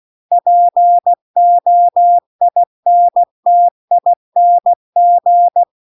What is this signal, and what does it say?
Morse code, heard as a keyed tone, about 16 wpm, 700 Hz. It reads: POINTING